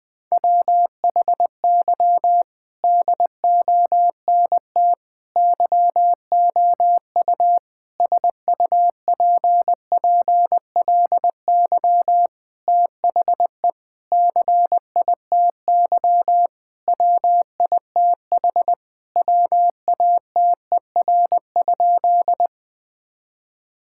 Morse code, heard as a keyed tone, about 21 wpm, 700 Hz. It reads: WHY DONT YOU SUPPLY THE CITY WITH WATER?
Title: Sentences from Top 1000 Words - Review Part 5 - 20wpm